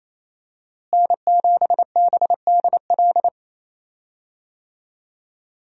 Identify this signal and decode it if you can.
Morse code, heard as a keyed tone, about 28 wpm, 700 Hz. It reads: N7BDL
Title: USA Call Signs - 5 Characters - Repeated 3x - 28wpm